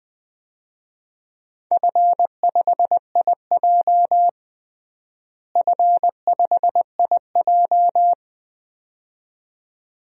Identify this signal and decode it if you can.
Morse code, heard as a keyed tone, about 19 wpm, 700 Hz. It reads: F5IJ F5IJ
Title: DXCC-100 Call Signs - 4 Characters - Repeated 2x - 20wpm